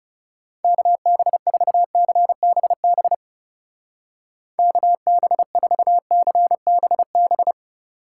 Morse code, heard as a keyed tone, about 37 wpm, 700 Hz. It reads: KB4CBB KB4CBB
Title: Speed Racing - USA 2x3 Call Signs 30wpm